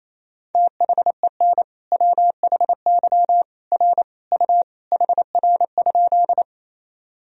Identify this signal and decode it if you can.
Morse code, heard as a keyed tone, about 28 wpm, 700 Hz. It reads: THEN WHY R U HR?
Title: Sentences from Top 200 Words and Abbreviations 28wpm